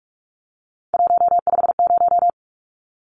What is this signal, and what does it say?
Morse code, heard as a keyed tone, about 44 wpm, 700 Hz. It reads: 150
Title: Numbers - 3-Digit 45wpm